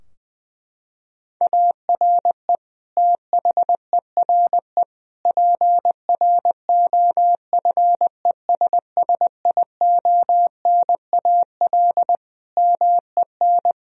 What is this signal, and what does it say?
Morse code, heard as a keyed tone, about 21 wpm, 700 Hz. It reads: ARE THERE PROFESSIONAL MEN
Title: Sentences from Top 1000 Words - Review Part 4 - 20wpm